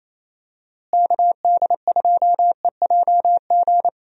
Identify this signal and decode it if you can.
Morse code, heard as a keyed tone, about 28 wpm, 700 Hz. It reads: KD2EJG
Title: USA Call Signs - 6 Characters - Repeated 3x - 28wpm